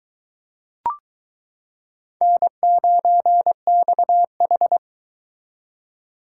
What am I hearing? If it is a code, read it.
Morse code, heard as a keyed tone, about 22 wpm, 700 Hz. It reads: N9XH